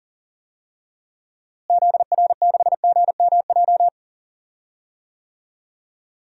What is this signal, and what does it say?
Morse code, heard as a keyed tone, about 40 wpm, 700 Hz. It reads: ZR6GMJ
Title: DXCC-100 Call Signs - 6 Characters - Repeated 3x - 40wpm